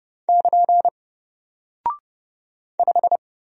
Morse code, heard as a keyed tone, about 30 wpm, 700 Hz. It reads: <KN> 5
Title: Single Character (Letter:Number:Punctuation:Prosign) 30wpm